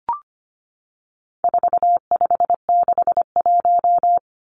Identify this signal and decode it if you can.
Morse code, heard as a keyed tone, about 25 wpm, 700 Hz. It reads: 4561
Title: Numbers - 4-Digit 25wpm